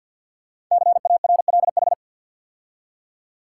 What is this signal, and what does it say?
Morse code, heard as a keyed tone, about 48 wpm, 700 Hz. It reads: CARLH
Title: Code Groups - 5 Letters - 50wpm